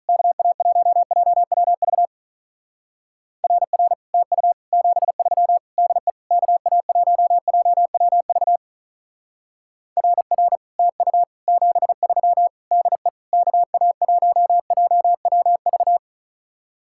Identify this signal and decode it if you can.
Morse code, heard as a keyed tone, about 45 wpm, 700 Hz. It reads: KA1JWV RR TU 73 DE KA1JWV RR TU 73 DE KA1JWV